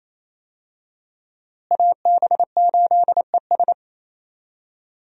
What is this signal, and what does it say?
Morse code, heard as a keyed tone, about 28 wpm, 700 Hz. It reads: AB8ES
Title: USA Call Signs - 5 Characters - Repeated 2x - 28wpm